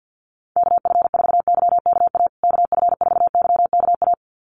Morse code, heard as a keyed tone, about 50 wpm, 700 Hz. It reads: KF4YKA KF4YKA
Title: USA Call Signs Repeated 2x 50wpm